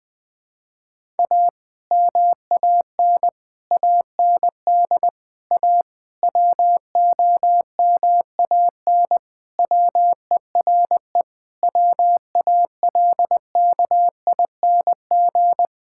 Morse code, heard as a keyed tone, about 21 wpm, 700 Hz. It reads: A MAN AND A WOMAN WERE WALKING
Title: Sentences with 5-Word Phrases - 20wpm